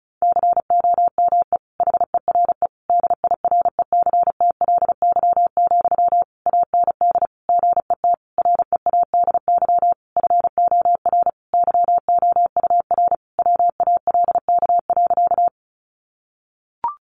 Morse code, heard as a keyed tone, about 34 wpm, 700 Hz. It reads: COME HERE DIRECTLY, AND GET READY FOR YOUR WALK.
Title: Sentences from Top 800 Words 35wpm